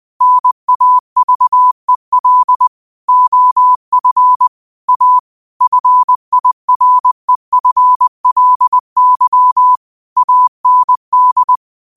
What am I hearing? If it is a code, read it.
Morse code, heard as a keyed tone, about 20 wpm, 1000 Hz. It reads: NAVEL OF A FIREFLY AND